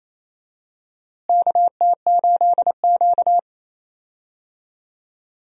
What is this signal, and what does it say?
Morse code, heard as a keyed tone, about 28 wpm, 700 Hz. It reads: KT8Q
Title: USA Call Signs - 4 Characters - Repeated 2x - 28wpm